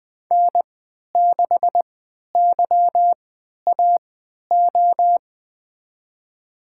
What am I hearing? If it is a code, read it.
Morse code, heard as a keyed tone, about 20 wpm, spaced as for 11 wpm, 700 Hz. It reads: N6YAO
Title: USA Call Signs - 3x Character Spacing - 20wpm